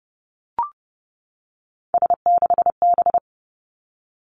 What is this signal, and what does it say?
Morse code, heard as a keyed tone, about 30 wpm, 700 Hz. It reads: S6B
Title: Three Challenging Letter-Number Combinations 30wpm